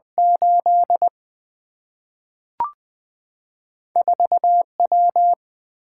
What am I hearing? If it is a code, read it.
Morse code, heard as a keyed tone, about 20 wpm, 700 Hz. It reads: 8 4W
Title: Single-Letter Single-Number 20wpm